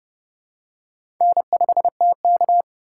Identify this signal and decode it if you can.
Morse code, heard as a keyed tone, about 30 wpm, 700 Hz. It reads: N5TK
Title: USA Call Signs - 4 Characters - 2x Character Spacing - 30wpm